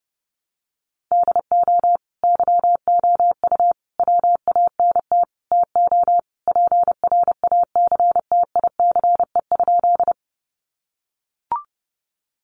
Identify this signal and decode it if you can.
Morse code, heard as a keyed tone, about 30 wpm, 700 Hz. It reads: DO YOU WANT TO PRACTICE?
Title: Sentences from Top 700 Words 30wpm